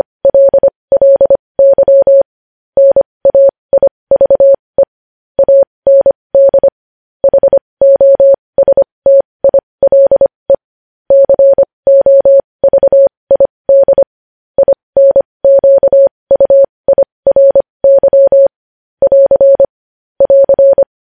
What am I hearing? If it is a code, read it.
Morse code, heard as a keyed tone, about 25 wpm, spaced as for 20 wpm, 550 Hz. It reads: LLY NAIVE AND HOSTILE COVID INQUIRY <AR> <AR>
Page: news-headlines-in-morse-code-at-20-wpm